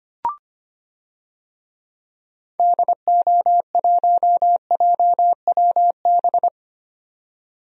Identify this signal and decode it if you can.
Morse code, heard as a keyed tone, about 25 wpm, 700 Hz. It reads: DO1JWB
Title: DXCC-100 Call Signs - 6 Characters - 25wpm